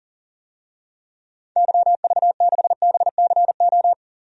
Morse code, heard as a keyed tone, about 40 wpm, 700 Hz. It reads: YV6BCO
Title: DXCC-100 Call Signs - 6 Characters - 40wpm